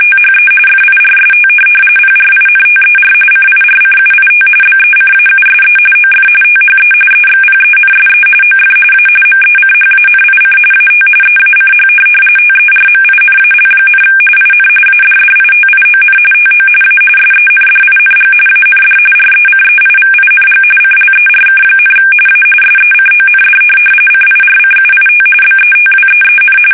RTTY, about 45.45 baud. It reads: THE FIRST PRINTER CIRCUIT BECAUSE OF THE EXPENSE OF A MECHANIC AT EACH TERMINAL, IN ADDITION TO A SENDING AND RECEIVING CLERK, A SECOND CIRCUIT BETWEEN
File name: fsk850